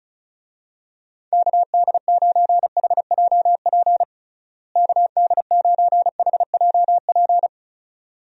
Morse code, heard as a keyed tone, about 34 wpm, 700 Hz. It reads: KD9HJP KD9HJP
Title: USA Call Signs - 6 Characters - Repeated 2x - 35wpm